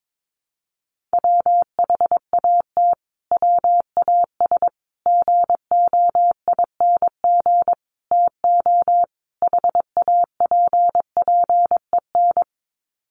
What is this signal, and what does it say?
Morse code, heard as a keyed tone, about 22 wpm, 700 Hz. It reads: WHAT WAS GOING TO HAPPEN
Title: Sets of 5 Words Encore 22wpm